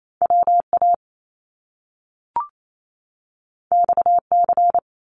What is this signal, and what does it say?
Morse code, heard as a keyed tone, about 28 wpm, 700 Hz. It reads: WA XC
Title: Top Two Letter Combinations in Words 28wpm